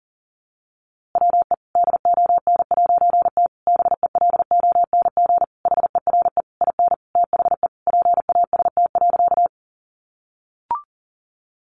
Morse code, heard as a keyed tone, about 40 wpm, 700 Hz. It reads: WE DON'T BELONG HERE IN THE PAST.
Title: A Sound Of Thunder 40wpm